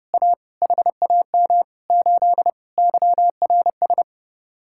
Morse code, heard as a keyed tone, about 31 wpm, 700 Hz. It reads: A HAM 8 YRS